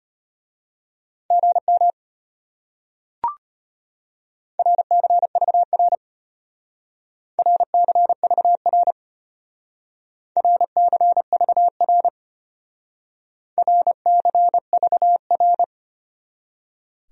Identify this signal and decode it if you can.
Morse code, heard as a keyed tone, about 33 wpm, 700 Hz. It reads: GM RCVR RCVR RCVR RCVR